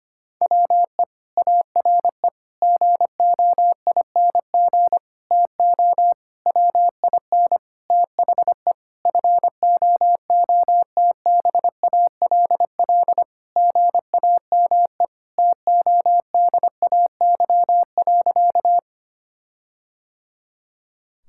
Morse code, heard as a keyed tone, about 25 wpm, 700 Hz. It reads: WE ARE GOING TO WIN THE FOOTBALL GAME TODAY.
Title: Sentences from Top 1600 Words 25wpm